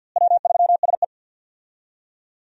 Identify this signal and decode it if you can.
Morse code, heard as a keyed tone, about 50 wpm, 700 Hz. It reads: W3SE
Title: USA Call Signs - 4 Characters - Repeated 2x - 50wpm